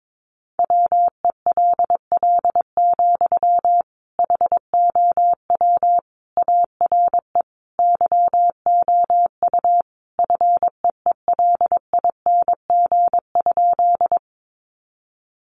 Morse code, heard as a keyed tone, about 22 wpm, 700 Hz. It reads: WELL, HOW ARE YOU FEELING?
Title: Sentences from Top 1200 Words 22wpm